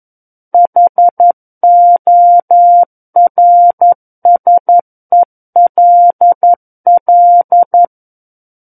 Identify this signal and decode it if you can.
Morse code, heard as a keyed tone, about 11 wpm, 700 Hz. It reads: HORSELL
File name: Book One / 02-Chapter 2 - 11 WPM